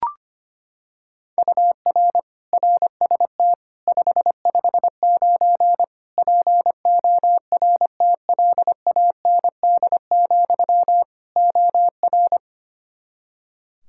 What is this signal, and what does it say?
Morse code, heard as a keyed tone, about 25 wpm, 700 Hz. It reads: UR RST 559 PORTLAND, OR
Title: K1USN SST QSO Practice 25wpm